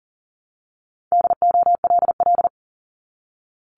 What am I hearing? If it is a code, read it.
Morse code, heard as a keyed tone, about 40 wpm, 690 Hz. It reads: DOLL
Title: Multi-letter Phonemes Repeated 3x 40wpm